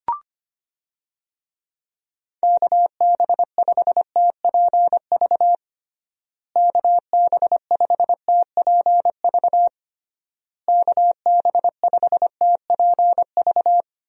Morse code, heard as a keyed tone, about 25 wpm, 700 Hz. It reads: KB5TPV KB5TPV KB5TPV
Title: USA Call Signs - 6 Characters - Repeated 3x - 25wpm